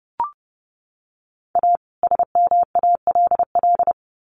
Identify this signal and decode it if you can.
Morse code, heard as a keyed tone, about 30 wpm, 700 Hz. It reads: A SMALL
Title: Sets of 2 Words 30wpm